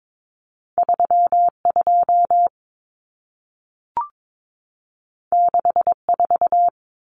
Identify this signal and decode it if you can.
Morse code, heard as a keyed tone, about 22 wpm, 700 Hz. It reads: 32 64
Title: Common Cut Numbers - 2-Digit 22wpm